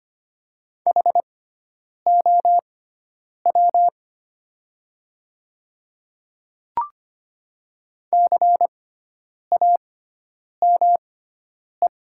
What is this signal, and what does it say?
Morse code, heard as a keyed tone, about 25 wpm, spaced as for 9 wpm, 700 Hz. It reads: HOW CAME
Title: 6x Character Spacing - Top 300 Words - 25wpm